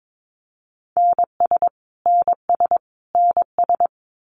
Morse code, heard as a keyed tone, about 22 wpm, 700 Hz. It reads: NS NS NS
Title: Canadian Provinces and Territories - Repeated 3x - 22wpm